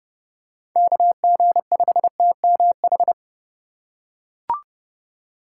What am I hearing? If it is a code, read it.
Morse code, heard as a keyed tone, about 30 wpm, 700 Hz. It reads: KG5TMH